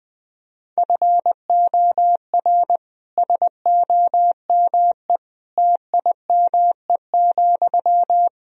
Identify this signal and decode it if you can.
Morse code, heard as a keyed tone, about 20 wpm, 700 Hz. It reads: FOR SOME TIME,
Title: Sentences from Top 1000 Words - Review Part 2 - 20wpm